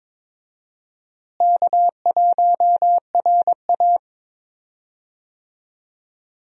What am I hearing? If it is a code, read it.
Morse code, heard as a keyed tone, about 22 wpm, 700 Hz. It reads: K1RA
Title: USA Call Signs - 4 Characters - Repeated 2x - 22wpm